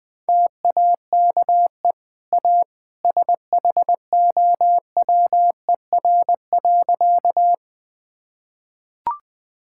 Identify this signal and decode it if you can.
Morse code, heard as a keyed tone, about 20 wpm, 700 Hz. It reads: TAKE A SHOWER.
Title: Easy Sentences with Hints - 20wpm